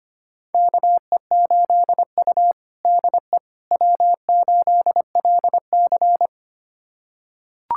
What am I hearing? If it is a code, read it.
Morse code, heard as a keyed tone, about 25 wpm, 700 Hz. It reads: KE8U DE W8LC
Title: USA Callsign DE Callsign - 4 Characters - 25wpm